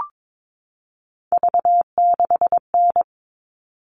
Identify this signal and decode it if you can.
Morse code, heard as a keyed tone, about 22 wpm, 700 Hz. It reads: V6N
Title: All Cut Numbers - 3-Digit 22wpm